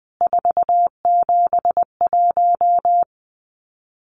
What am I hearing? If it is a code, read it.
Morse code, heard as a keyed tone, about 20 wpm, 700 Hz. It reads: 471
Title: CWT Exchange 20wpm